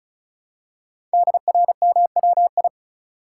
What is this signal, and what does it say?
Morse code, heard as a keyed tone, about 36 wpm, 700 Hz. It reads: DRMWI
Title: Code Groups - 5 Letters - 35wpm